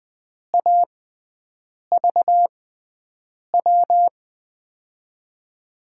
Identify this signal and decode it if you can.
Morse code, heard as a keyed tone, about 20 wpm, spaced as for 7 wpm, 700 Hz. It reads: AVW
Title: USA Call Signs - 6x Character Spacing - 20wpm